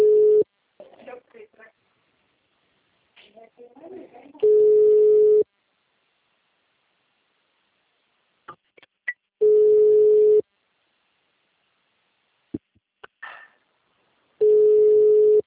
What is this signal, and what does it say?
Ringback tone (Europe, 425 Hz), with speech.